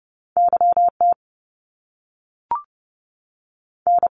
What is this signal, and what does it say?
Morse code, heard as a keyed tone, about 30 wpm, 700 Hz. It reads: YT N
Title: Canadian Provinces and Territories 30wpm